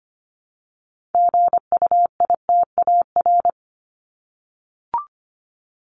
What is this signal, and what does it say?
Morse code, heard as a keyed tone, about 25 wpm, 700 Hz. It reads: GUITAR
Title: Top 1501-1750 Words 25wpm